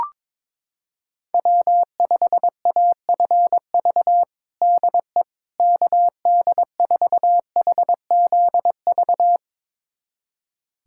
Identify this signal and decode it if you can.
Morse code, heard as a keyed tone, about 22 wpm, 700 Hz. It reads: W5AFV DE KD4HZV
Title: USA Callsign DE Callsign 22wpm